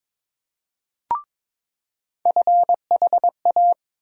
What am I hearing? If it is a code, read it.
Morse code, heard as a keyed tone, about 22 wpm, 700 Hz. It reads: FHA
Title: USA Call Sign Postfixes - 3 Letters 22wpm